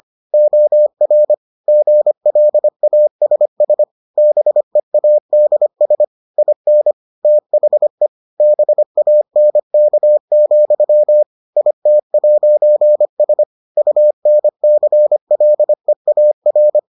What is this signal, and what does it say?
Morse code, heard as a keyed tone, about 25 wpm, 600 Hz. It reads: OR GLASS BEADS IN THE BANK, IT'S UNCLEAR